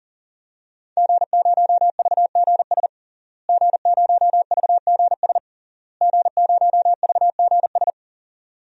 Morse code, heard as a keyed tone, about 37 wpm, 700 Hz. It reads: G0VGS G0VGS G0VGS
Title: DXCC-100 Call Signs - 5 Characters - Repeated 3x - 40wpm